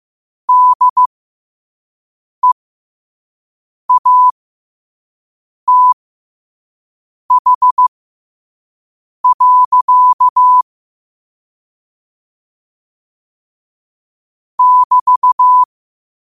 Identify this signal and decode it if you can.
Morse code, heard as a keyed tone, about 15 wpm, spaced as for 5 wpm, 1000 Hz. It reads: DEATH. <BT>
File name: qotdfiles / 5wpm